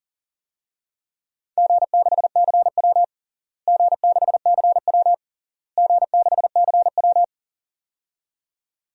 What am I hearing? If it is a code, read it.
Morse code, heard as a keyed tone, about 37 wpm, 700 Hz. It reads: G6CW G6CW G6CW
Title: DXCC-100 Call Signs - 4 Characters - Repeated 3x - 40wpm